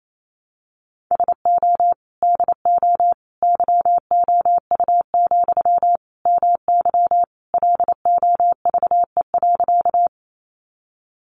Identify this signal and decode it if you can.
Morse code, heard as a keyed tone, about 28 wpm, 700 Hz. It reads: SO DO YOU, MY LOVE.
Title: Sentences from Top 400 Words 28wpm